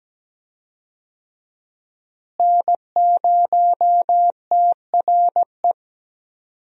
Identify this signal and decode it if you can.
Morse code, heard as a keyed tone, about 17 wpm, 700 Hz. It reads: N0TRE
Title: USA Call Signs - 5 Characters - Repeated 2x - 17wpm